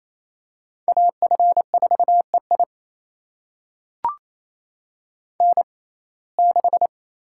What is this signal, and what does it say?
Morse code, heard as a keyed tone, about 28 wpm, spaced as for 20 wpm, 700 Hz. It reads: AF4EI N6
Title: USA Call Signs - 5 Characters - 6x Character Spacing - 28wpm